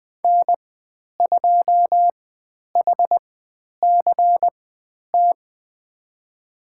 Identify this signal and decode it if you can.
Morse code, heard as a keyed tone, about 20 wpm, spaced as for 10 wpm, 700 Hz. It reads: N2HCT
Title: 102 Lesson 17g - Letter u - Call Signs 20wpm